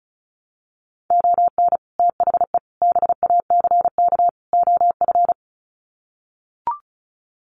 Morse code, heard as a keyed tone, about 34 wpm, 700 Hz. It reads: ON THE BACK OF
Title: Sets of 4 Words 35wpm